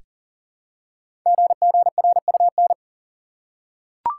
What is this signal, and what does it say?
Morse code, heard as a keyed tone, about 39 wpm, 690 Hz. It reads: GGRUN
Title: Code Groups - 5 Letters - 40wpm